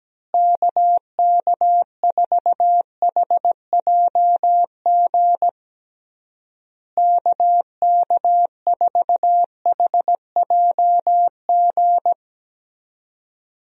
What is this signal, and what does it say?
Morse code, heard as a keyed tone, about 17 wpm, 700 Hz. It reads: KK4HJG KK4HJG
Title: USA Call Signs - 6 Characters - Repeated 3x - 17wpm